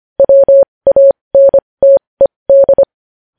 Morse code, heard as a keyed tone, about 25 wpm, spaced as for 20 wpm, 550 Hz. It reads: WANTED
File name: podcast / News headlines-1x-25x20